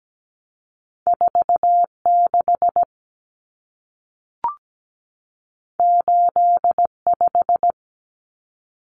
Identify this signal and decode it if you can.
Morse code, heard as a keyed tone, about 17 wpm, 700 Hz. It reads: 46 85
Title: Common Cut Numbers - 2-Digit 17wpm